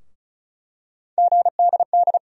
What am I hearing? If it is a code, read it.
Morse code, heard as a keyed tone, about 34 wpm, 700 Hz. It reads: GDD
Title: USA Call Sign Postfixes - 3 Letters 35wpm